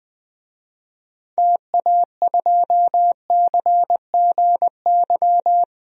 Morse code, heard as a keyed tone, about 20 wpm, 700 Hz. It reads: TA2CGY
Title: DXCC-100 Call Signs - 6 Characters - Repeated 2x - 20wpm